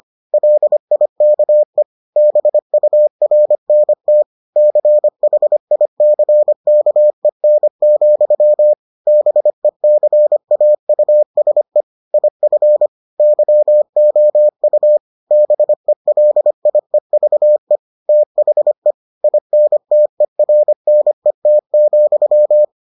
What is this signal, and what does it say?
Morse code, heard as a keyed tone, about 25 wpm, 600 Hz. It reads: LIKE BURNT CHICKEN, BECAUSE IF YOU BELIEVE THE INTERNET,